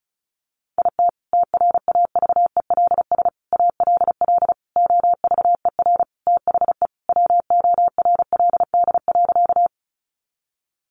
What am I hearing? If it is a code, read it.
Morse code, heard as a keyed tone, about 34 wpm, 700 Hz. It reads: IT TRAVELS ALL OVER THE WORLD.
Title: Additional Easy Phrases 35wpm